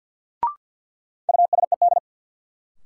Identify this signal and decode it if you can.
Morse code, heard as a keyed tone, about 52 wpm, 700 Hz. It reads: USED
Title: Sentences from Top 600 Words - New Word Drill - 50wpm